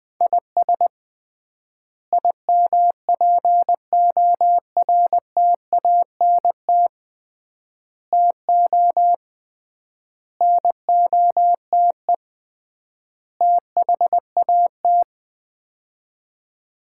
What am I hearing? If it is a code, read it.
Morse code, heard as a keyed tone, about 20 wpm, 700 Hz. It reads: IS IMPORTANT TO NOTE THAT
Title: 3x Word-Spacing - Sets of 5 Words - 20wpm